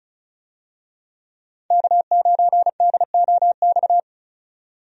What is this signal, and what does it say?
Morse code, heard as a keyed tone, about 34 wpm, 700 Hz. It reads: K9DOX
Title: USA Call Signs - 5 Characters - 35wpm